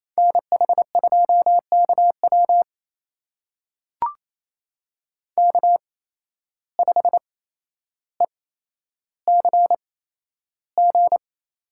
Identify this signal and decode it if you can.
Morse code, heard as a keyed tone, about 28 wpm, spaced as for 12 wpm, 700 Hz. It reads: NH2KW K5ECG
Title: USA Call Signs - 5 Characters - 8x Character Spacing - 28wpm